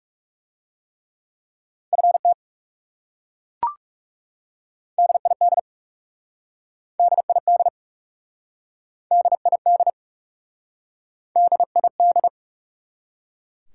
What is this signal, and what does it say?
Morse code, heard as a keyed tone, about 40 wpm, 700 Hz. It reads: FT DID DID DID DID